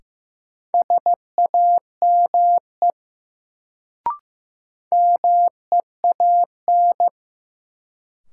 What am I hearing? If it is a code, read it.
Morse code, heard as a keyed tone, about 16 wpm, 700 Hz. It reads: SAME MEAN
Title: Sentences from Top 200 Words - New Word Drill - 15wpm